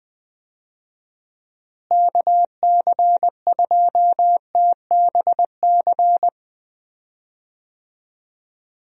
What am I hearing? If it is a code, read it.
Morse code, heard as a keyed tone, about 21 wpm, 700 Hz. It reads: KC2TBC
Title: USA Call Signs - 6 Characters - Repeated 3x - 20wpm